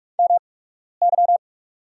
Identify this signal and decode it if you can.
Morse code, heard as a keyed tone, about 44 wpm, spaced as for 12 wpm, 700 Hz. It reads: MY
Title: 8x Character Spacing - Top 300 Words - 45wpm